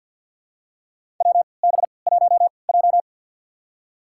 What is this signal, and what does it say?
Morse code, heard as a keyed tone, about 52 wpm, spaced as for 29 wpm, 700 Hz. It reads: WB1J
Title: USA Call Signs - 4 Characters - 3x Character Spacing - 50wpm